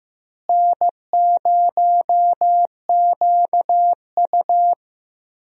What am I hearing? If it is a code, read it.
Morse code, heard as a keyed tone, about 15 wpm, 700 Hz. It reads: N0QU
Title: USA Call Signs - 4 Characters - Repeated 2x - 15wpm